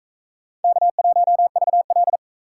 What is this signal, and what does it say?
Morse code, heard as a keyed tone, about 38 wpm, 700 Hz. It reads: K1VL